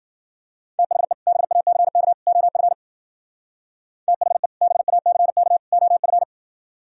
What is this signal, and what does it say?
Morse code, heard as a keyed tone, about 52 wpm, 700 Hz. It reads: THE BACK OF THE BACK OF